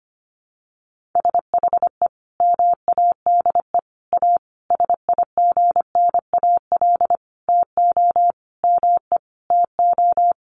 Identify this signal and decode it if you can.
Morse code, heard as a keyed tone, about 25 wpm, 700 Hz. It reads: SHE MADE A SIGNAL TO ME TO
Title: Sentences from Top 1400 Words 25wpm